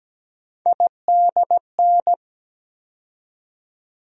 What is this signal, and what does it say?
Morse code, heard as a keyed tone, about 17 wpm, 700 Hz. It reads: IDN
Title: USA Call Sign Postfixes - 3 Letters 17wpm